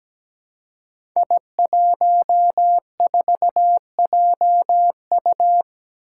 Morse code, heard as a keyed tone, about 17 wpm, 700 Hz. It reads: I14JU
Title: Five Challenging Letter-Number Combinations 17wpm